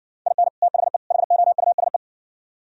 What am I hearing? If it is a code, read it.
Morse code, heard as a keyed tone, about 56 wpm, 700 Hz. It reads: IN THE HOUSE